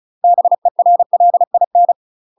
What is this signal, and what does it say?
Morse code, heard as a keyed tone, about 34 wpm, 700 Hz. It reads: BERLIN
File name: UK MASTER QSOs TRAINING / 35WPM / UK MASTER QSOs 0024